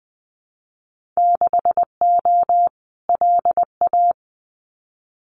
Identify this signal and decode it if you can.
Morse code, heard as a keyed tone, about 20 wpm, 700 Hz. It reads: 6O LA
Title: Winter Field Day Exchanges Encore 20wpm